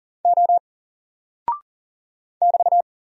Morse code, heard as a keyed tone, about 40 wpm, 700 Hz. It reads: O <BT>
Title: Single Character (Letter:Number:Punctuation:Prosign) Alt 40wpm